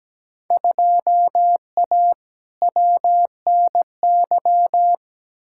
Morse code, heard as a keyed tone, about 17 wpm, 700 Hz. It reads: 2A WNY
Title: Field Day Exchanges - 17wpm